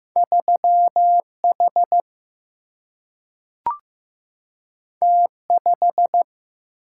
Morse code, heard as a keyed tone, about 15 wpm, 700 Hz. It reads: 3H T5
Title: Single-Letter Single-Number 15wpm